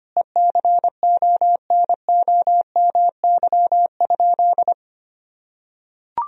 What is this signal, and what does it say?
Morse code, heard as a keyed tone, about 25 wpm, 700 Hz. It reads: ECONOMY?
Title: Sentences from Top 1000 Words - Review Part 3 - 25wpm